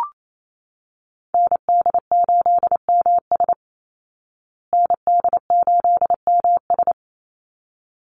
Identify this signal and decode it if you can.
Morse code, heard as a keyed tone, about 28 wpm, 700 Hz. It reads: ND8MS ND8MS
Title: Call Signs Repeated 3x - 28wpm